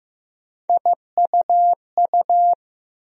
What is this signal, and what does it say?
Morse code, heard as a keyed tone, about 15 wpm, 700 Hz. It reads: IUU
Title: USA Call Sign Postfixes - 3 Letters 15wpm